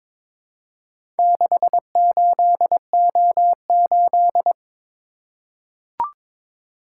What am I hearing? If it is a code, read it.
Morse code, heard as a keyed tone, about 22 wpm, 700 Hz. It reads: 68O8
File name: practice / 22wpm / Four Letter-Number Combinations - 22wpm